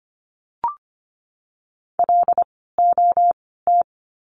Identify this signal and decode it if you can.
Morse code, heard as a keyed tone, about 25 wpm, 700 Hz. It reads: L O T
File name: course / 25:10wpm / 094 Lesson 16f - Letter c - Words 25wpm